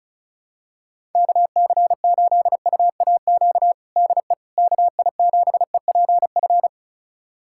Morse code, heard as a keyed tone, about 36 wpm, 700 Hz. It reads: KC8UAQ DE KI7EPF
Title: USA Callsign DE Callsign - 6 Characters - 35wpm